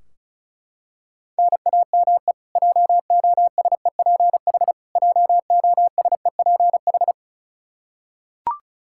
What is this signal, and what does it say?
Morse code, heard as a keyed tone, about 36 wpm, 700 Hz. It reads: NAME JOSEPH JOSEPH
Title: Practice QSO - Exchange 1 - Beginning - 35wpm